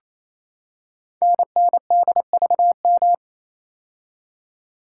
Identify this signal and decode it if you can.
Morse code, heard as a keyed tone, about 28 wpm, 700 Hz. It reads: NNDVM
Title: Code Groups - 5 Letters - 28wpm